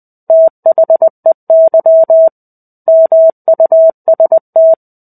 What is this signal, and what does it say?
Morse code, heard as a keyed tone, about 20 wpm, 650 Hz. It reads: THEY MUST
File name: 05-Chapter-05 20 WPM